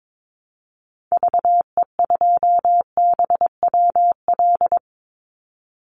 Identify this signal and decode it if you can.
Morse code, heard as a keyed tone, about 22 wpm, 700 Hz. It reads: VE2BWL
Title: Call Signs Repeated 2x - 22wpm